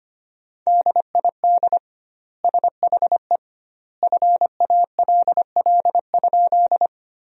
Morse code, heard as a keyed tone, about 25 wpm, 700 Hz. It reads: DID SHE FALL?
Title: 2x Word-Spacing - Sentences from Top 500 Words - 25wpm